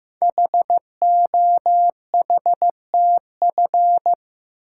Morse code, heard as a keyed tone, about 15 wpm, 700 Hz. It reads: HOHTF
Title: Code Groups - 5 Letters - 15wpm